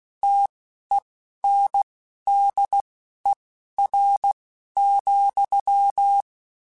Morse code, heard as a keyed tone, about 16 wpm, spaced as for 11 wpm, 800 Hz. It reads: TENDER,